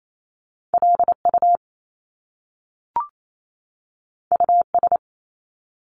Morse code, heard as a keyed tone, about 28 wpm, 700 Hz. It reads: LU US
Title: Top Two Letter Combinations in Words 28wpm